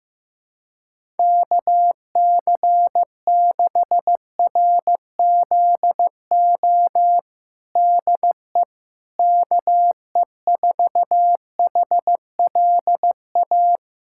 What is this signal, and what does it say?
Morse code, heard as a keyed tone, about 15 wpm, 700 Hz. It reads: KC6RZO DE KE4HLA
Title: USA Callsign DE Callsign - 6 Characters - 15wpm